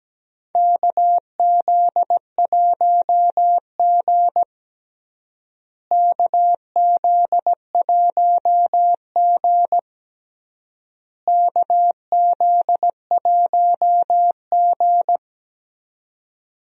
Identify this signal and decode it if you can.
Morse code, heard as a keyed tone, about 17 wpm, 700 Hz. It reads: KZ1G KZ1G KZ1G
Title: USA Call Signs - 4 Characters - Repeated 3x - 17wpm